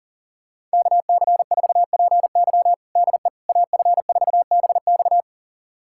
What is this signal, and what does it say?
Morse code, heard as a keyed tone, about 37 wpm, 700 Hz. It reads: KC4PY DE AF4BX
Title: USA Callsign DE Callsign - 5 Characters - 40wpm